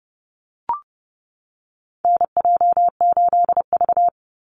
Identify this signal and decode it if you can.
Morse code, heard as a keyed tone, about 30 wpm, 700 Hz. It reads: NJ8V
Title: Call Signs - 30wpm